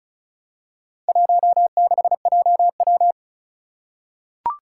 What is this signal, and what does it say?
Morse code, heard as a keyed tone, about 36 wpm, 700 Hz. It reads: 16JW
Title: Four Challenging Letter-Number Combinations 35wpm